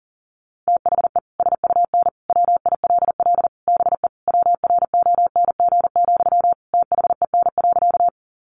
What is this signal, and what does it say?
Morse code, heard as a keyed tone, about 40 wpm, 700 Hz. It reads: THE SUN WILL BE WRONG, THEN.
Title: Sentences from Top 900 Words 40wpm